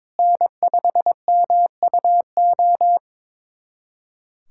Morse code, heard as a keyed tone, about 22 wpm, 700 Hz. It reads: N5MUO
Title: USA Callsign DE Callsign 22wpm